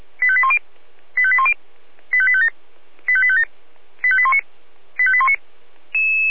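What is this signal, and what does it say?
Five-tone selective call (ZVEI), heard as quick runs of five tones, tones of about 70 ms. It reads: 85810 85810 85857 85857 85819 85819
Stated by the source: ZVEI five-tone selective call received on an SDR.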